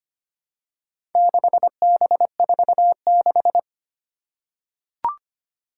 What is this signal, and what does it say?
Morse code, heard as a keyed tone, about 25 wpm, 700 Hz. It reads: 6B46
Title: Four Challenging Letter-Number Combinations 25wpm